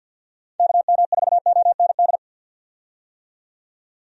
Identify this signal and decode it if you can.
Morse code, heard as a keyed tone, about 51 wpm, 700 Hz. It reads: KM4OND
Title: USA Call Signs - 6 Characters - 50wpm